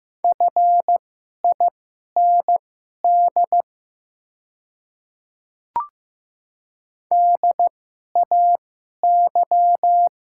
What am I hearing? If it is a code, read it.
Morse code, heard as a keyed tone, about 15 wpm, spaced as for 11 wpm, 690 Hz. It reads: FIND DAY
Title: 2x Character Spacing - Top 200 Words - 15wpm